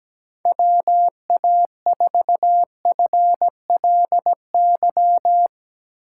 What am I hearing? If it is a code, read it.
Morse code, heard as a keyed tone, about 17 wpm, 700 Hz. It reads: WA4FLY
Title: USA Call Signs - 6 Characters - 4x Character Spacing - 17wpm